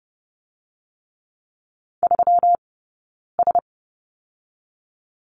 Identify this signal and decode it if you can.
Morse code, heard as a keyed tone, about 30 wpm, 700 Hz. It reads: 3 S
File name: Lesson 20d - Number 3 - Two Characters 30wpm